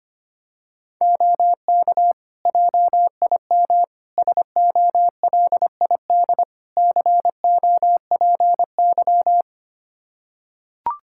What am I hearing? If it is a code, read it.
Morse code, heard as a keyed tone, about 25 wpm, 700 Hz. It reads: OK JIM SOLID COPY
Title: Practice QSO - Exchange 3 - More Info - 25wpm